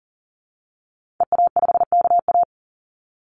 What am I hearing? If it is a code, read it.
Morse code, heard as a keyed tone, about 40 wpm, 700 Hz. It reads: EA5KA
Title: Call Signs Repeated 2x - 40wpm